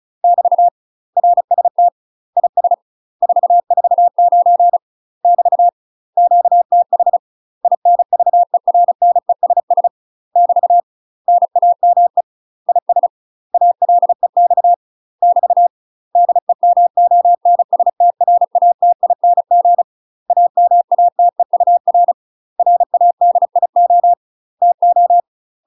Morse code, heard as a keyed tone, about 34 wpm, 700 Hz. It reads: <BT> RST IS 449 <BT> QTH INVERNESS <BT> NAME IS ALEX <BT> DEMONSTRATING AMATEUR RADIO TO
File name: UK MASTER QSOs TRAINING / 35WPM / UK MASTER QSOs 0024